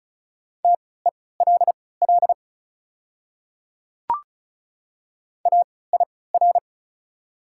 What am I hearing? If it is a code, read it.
Morse code, heard as a keyed tone, about 36 wpm, spaced as for 20 wpm, 700 Hz. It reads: TELL AIR
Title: 3x Character Spacing - Top 200 Words - 35wpm